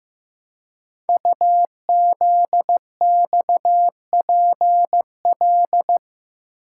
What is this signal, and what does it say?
Morse code, heard as a keyed tone, about 15 wpm, 700 Hz. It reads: UZXPL
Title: Code Groups - 5 Letters - 15wpm